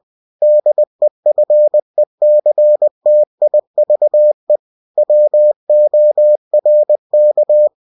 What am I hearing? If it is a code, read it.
Morse code, heard as a keyed tone, about 20 wpm, 600 Hz. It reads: DEFECTIVE WORK
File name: cw-books / winnie-the-pooh / winnie-20wpm-page-0119